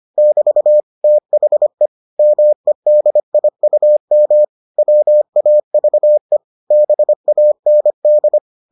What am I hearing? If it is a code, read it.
Morse code, heard as a keyed tone, about 25 wpm, 600 Hz. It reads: <BT> THE MEDIUM WAVE BAND